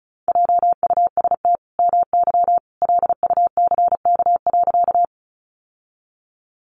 Morse code, heard as a keyed tone, about 34 wpm, 700 Hz. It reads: JUST MY LUCK.
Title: Sentences from Top 1500 Words 35wpm